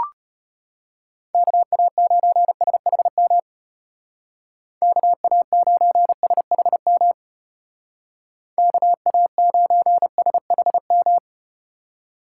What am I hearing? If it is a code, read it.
Morse code, heard as a keyed tone, about 35 wpm, 700 Hz. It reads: KA9SHM KA9SHM KA9SHM